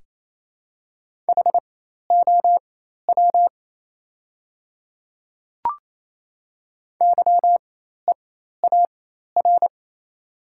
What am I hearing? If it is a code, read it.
Morse code, heard as a keyed tone, about 27 wpm, spaced as for 13 wpm, 700 Hz. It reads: HOW YEAR
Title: 4x Character Spacing - Top 300 Words - 28wpm